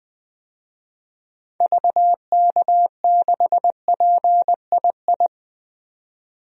Morse code, heard as a keyed tone, about 20 wpm, 700 Hz. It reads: VK6PII
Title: DXCC-100 Call Signs - 6 Characters - 20wpm